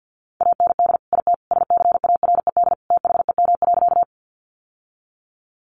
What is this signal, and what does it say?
Morse code, heard as a keyed tone, about 49 wpm, 700 Hz. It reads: AND IT SCARED THEM.
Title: Sentences from Top 700 Words 50wpm